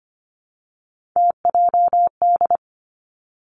Morse code, heard as a keyed tone, about 25 wpm, 700 Hz. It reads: TJD
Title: Three Challenging Letter-Number Combinations 25wpm